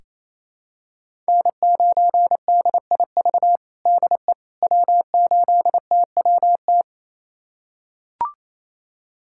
Morse code, heard as a keyed tone, about 28 wpm, 700 Hz. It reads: N9DIV DE W8TWT
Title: USA Callsign DE Callsign - 5 Characters - 28wpm